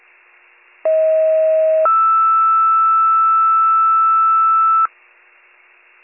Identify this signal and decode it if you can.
Two-tone page, heard as a one-second tone, then a three-second tone, about 640 then 1320 Hz. It is heard from 0.9 to 4.9 s.